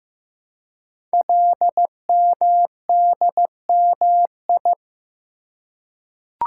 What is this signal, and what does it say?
Morse code, heard as a keyed tone, about 15 wpm, 700 Hz. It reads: LMDMI
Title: Code Groups - 5 Letters - 15wpm